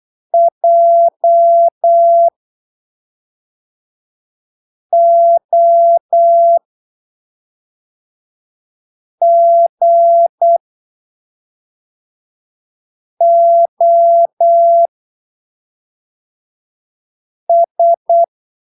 Morse code, heard as a keyed tone, about 8 wpm, spaced as for 3 wpm, 680 Hz. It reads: JOGOS
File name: arquivos / texto / texto 08 08ppm